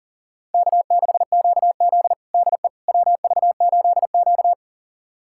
Morse code, heard as a keyed tone, about 40 wpm, 700 Hz. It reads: K6QZ DE WV8Q
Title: USA Callsign DE Callsign - 4 Characters - 40wpm